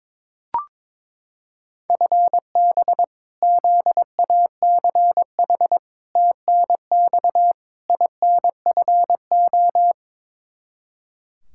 Morse code, heard as a keyed tone, about 22 wpm, 700 Hz. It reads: FB ZACH TNX INFO
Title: K1USN SST QSO Practice 22wpm